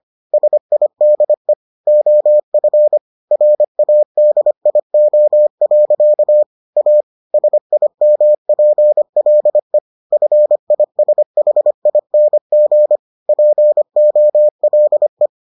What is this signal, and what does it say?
Morse code, heard as a keyed tone, about 25 wpm, 600 Hz. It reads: SIDE OF RADIO. A SIMPLE FISHING POLE